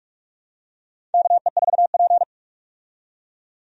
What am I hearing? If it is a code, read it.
Morse code, heard as a keyed tone, about 48 wpm, 700 Hz. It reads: KE4P
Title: USA Call Signs - 4 Characters - 45wpm